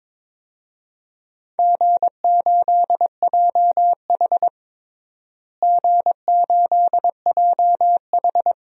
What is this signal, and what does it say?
Morse code, heard as a keyed tone, about 22 wpm, 700 Hz. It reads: G8JH G8JH
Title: DXCC-100 Call Signs - 4 Characters - Repeated 3x - 22wpm